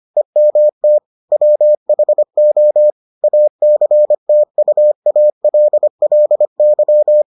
Morse code, heard as a keyed tone, about 25 wpm, 600 Hz. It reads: EMT WHO ACTUALLY